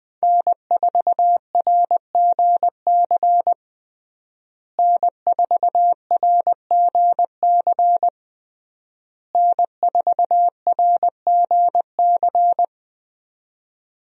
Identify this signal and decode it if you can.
Morse code, heard as a keyed tone, about 20 wpm, 700 Hz. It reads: N4RGC N4RGC N4RGC
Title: USA Call Signs - 5 Characters - Repeated 3x - 20wpm